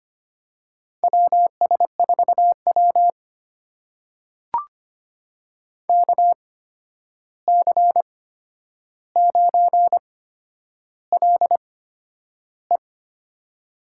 Morse code, heard as a keyed tone, about 25 wpm, spaced as for 10 wpm, 700 Hz. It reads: WS4W KC9LE
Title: USA Call Signs - 8x Character Spacing - 25wpm